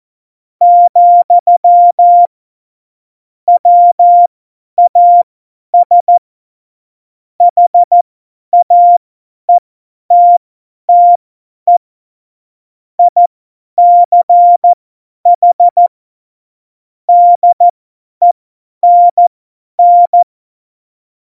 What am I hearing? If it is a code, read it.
Morse code, heard as a keyed tone, about 14 wpm, spaced as for 10 wpm, 700 Hz. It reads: , WAS HAETTE ICH DENN
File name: Morsekurs / Witze / Witz 22. September 2020, 50 F70